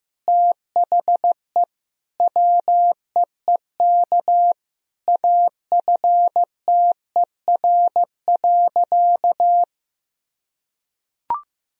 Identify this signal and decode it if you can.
Morse code, heard as a keyed tone, about 15 wpm, 700 Hz. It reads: THE WEEK AFTER.
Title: Sentences from Top 500 Words and Abbreviations 15wpm